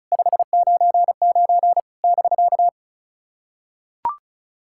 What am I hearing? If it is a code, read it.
Morse code, heard as a keyed tone, about 34 wpm, 700 Hz. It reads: 599 <BK>
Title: SOTA QSO Practice 35wpm